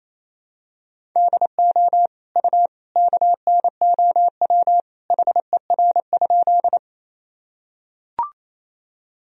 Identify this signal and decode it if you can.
Morse code, heard as a keyed tone, about 28 wpm, 700 Hz. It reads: DO U KNOW HER?
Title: Sentences from Top 100 Words and Abbreviations 28wpm